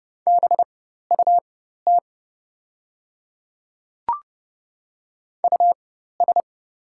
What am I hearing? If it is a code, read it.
Morse code, heard as a keyed tone, about 30 wpm, spaced as for 14 wpm, 700 Hz. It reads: BUT US
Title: 4x Character Spacing - Top 200 Words - 30wpm